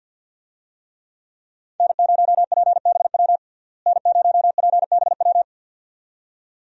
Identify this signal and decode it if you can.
Morse code, heard as a keyed tone, about 49 wpm, 700 Hz. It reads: N0PBW N0PBW
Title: USA Call Signs - 5 Characters - Repeated 2x - 50wpm